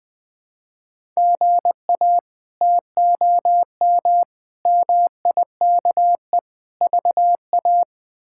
Morse code, heard as a keyed tone, about 20 wpm, 700 Hz. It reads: GA TOM MIKE VA
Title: K1USN SST QSO Practice - 20wpm